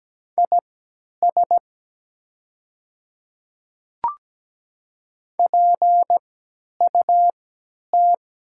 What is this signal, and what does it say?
Morse code, heard as a keyed tone, about 17 wpm, spaced as for 10 wpm, 700 Hz. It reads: IS PUT
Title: 3x Character Spacing - Top 300 Words - 17wpm